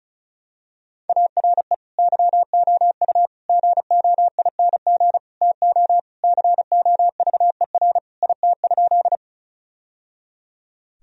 Morse code, heard as a keyed tone, about 34 wpm, 700 Hz. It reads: ARE YOU GOING TO COVER IT?
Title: Sentences from Top 1000 Words - Review Part 1 - 35wpm